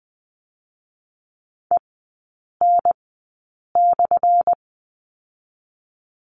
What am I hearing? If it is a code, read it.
Morse code, heard as a keyed tone, about 20 wpm, spaced as for 8 wpm, 700 Hz. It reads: EN/
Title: 178 Lesson 29e - Slash - Three Characters 20wpm